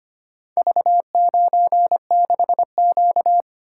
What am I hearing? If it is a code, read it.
Morse code, heard as a keyed tone, about 25 wpm, 700 Hz. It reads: V96Q
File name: practice / 25wpm / Four Letter-Number Combinations - 25wpm